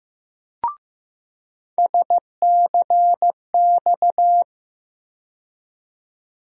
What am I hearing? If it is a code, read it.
Morse code, heard as a keyed tone, about 15 wpm, 700 Hz. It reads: SCX
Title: USA Call Sign Postfixes - 3 Letters 15wpm